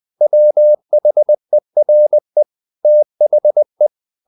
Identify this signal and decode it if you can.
Morse code, heard as a keyed tone, about 20 wpm, 600 Hz. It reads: WHERE THE
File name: the old man and the sea / the old man and the sea part 1 20wpm